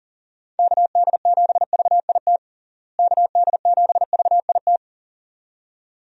Morse code, heard as a keyed tone, about 40 wpm, 700 Hz. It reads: KD7VIT KD7VIT
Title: USA Call Signs - 6 Characters - Repeated 2x - 40wpm